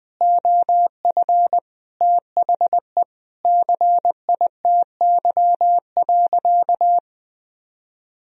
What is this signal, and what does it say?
Morse code, heard as a keyed tone, about 20 wpm, 700 Hz. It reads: OF THE CITY.
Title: Sentences from Top 1000 Words - Review Part 2 - 20wpm